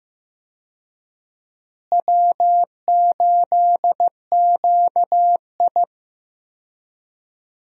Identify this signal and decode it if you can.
Morse code, heard as a keyed tone, about 15 wpm, 700 Hz. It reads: W8QI
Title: USA Call Signs - 4 Characters - Repeated 3x - 15wpm